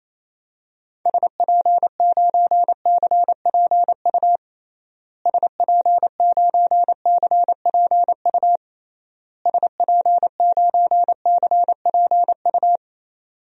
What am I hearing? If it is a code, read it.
Morse code, heard as a keyed tone, about 28 wpm, 700 Hz. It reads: SP9CPU SP9CPU SP9CPU
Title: DXCC-100 Call Signs - 6 Characters - Repeated 3x - 28wpm